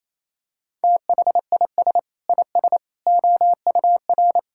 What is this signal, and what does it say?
Morse code, heard as a keyed tone, about 28 wpm, 700 Hz. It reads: THIS IS OUR
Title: Sentences from Top 1000 Words - Review Part 5 - 28wpm